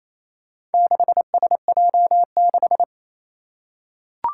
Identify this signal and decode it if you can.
Morse code, heard as a keyed tone, about 28 wpm, 700 Hz. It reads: 6SJ6
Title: Four Challenging Letter-Number Combinations 28wpm